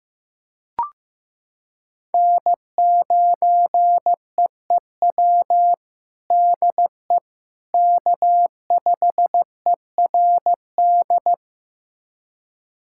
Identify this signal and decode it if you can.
Morse code, heard as a keyed tone, about 15 wpm, 700 Hz. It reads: N9EEW DE K5ERD
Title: USA Callsign DE Callsign - 5 Characters - 15wpm